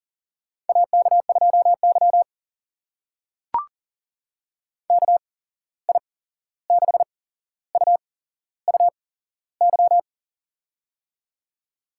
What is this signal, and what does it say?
Morse code, heard as a keyed tone, about 40 wpm, spaced as for 15 wpm, 700 Hz. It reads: AK2Y KI6UUY
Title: USA Call Signs - 8x Character Spacing - 40wpm